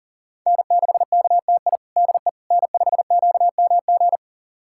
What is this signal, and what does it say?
Morse code, heard as a keyed tone, about 40 wpm, 700 Hz. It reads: N6KTI DE N5QMG
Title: USA Callsign DE Callsign - 5 Characters - 40wpm